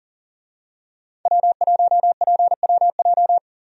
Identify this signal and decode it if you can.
Morse code, heard as a keyed tone, about 40 wpm, 700 Hz. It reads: W1PWJ
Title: USA Call Signs - 5 Characters - 40wpm